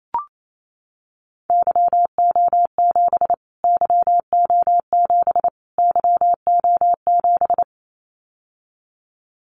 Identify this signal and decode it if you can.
Morse code, heard as a keyed tone, about 28 wpm, 700 Hz. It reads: YO7 YO7 YO7
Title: Callsign Prefixes for Top 25 DXCC - Repeated 3x - 28wpm